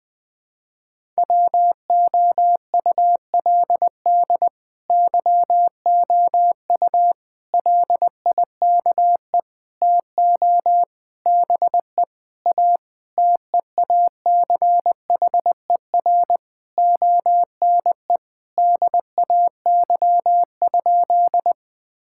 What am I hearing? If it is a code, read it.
Morse code, heard as a keyed tone, about 20 wpm, 700 Hz. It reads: WOULD YOU LIKE TO BE A TEACHER ONE DAY?
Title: Sentences from Top 1000 Words - Review Part 6 - 20wpm